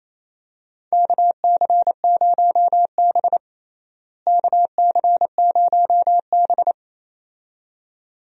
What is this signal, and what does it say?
Morse code, heard as a keyed tone, about 28 wpm, 700 Hz. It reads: KC0B KC0B
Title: USA Call Signs - 4 Characters - Repeated 2x - 28wpm